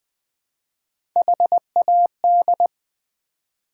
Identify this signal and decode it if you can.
Morse code, heard as a keyed tone, about 21 wpm, 700 Hz. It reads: HAD
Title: 4x Word-Spacing - Sets of 4 Words - 20wpm